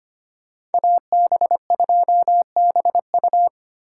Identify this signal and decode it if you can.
Morse code, heard as a keyed tone, about 32 wpm, 700 Hz. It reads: AB2BU